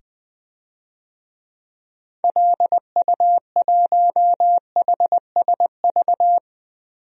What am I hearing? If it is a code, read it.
Morse code, heard as a keyed tone, about 21 wpm, 700 Hz. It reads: LU1HSV
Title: DXCC-100 Call Signs - 6 Characters - 20wpm